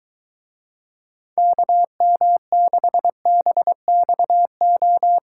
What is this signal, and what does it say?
Morse code, heard as a keyed tone, about 23 wpm, 700 Hz. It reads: KM6BXO